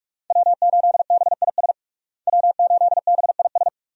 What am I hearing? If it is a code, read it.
Morse code, heard as a keyed tone, about 41 wpm, 700 Hz. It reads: W8BIS W8BIS
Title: USA Call Signs - 5 Characters - Repeated 3x - 45wpm